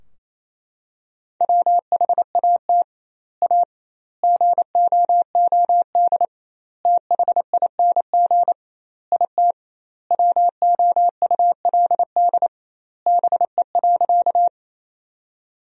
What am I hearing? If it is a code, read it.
Morse code, heard as a keyed tone, about 28 wpm, 700 Hz. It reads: WHAT A GOOD THING IT WOULD BE.
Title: 2x Word-Spacing - Sentences from Top 200 Words - 28wpm